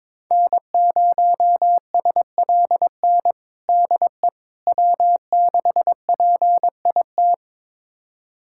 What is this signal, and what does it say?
Morse code, heard as a keyed tone, about 21 wpm, 700 Hz. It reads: N0SLN DE W6PIT
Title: USA Callsign DE Callsign - 5 Characters - 22wpm